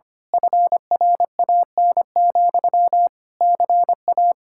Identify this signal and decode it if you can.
Morse code, heard as a keyed tone, about 24 wpm, 700 Hz. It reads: FRAN, CA
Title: Speed-Racing Common QSO Phrases - 25wpm